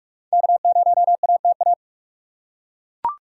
Morse code, heard as a keyed tone, about 42 wpm, 700 Hz. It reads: K0ATA
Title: USA Call Signs - 5 Characters - 4x Character Spacing - 45wpm